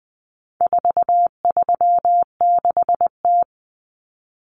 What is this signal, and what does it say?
Morse code, heard as a keyed tone, about 20 wpm, 700 Hz. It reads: 436T
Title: Common Cut Numbers - 4-Digit 20wpm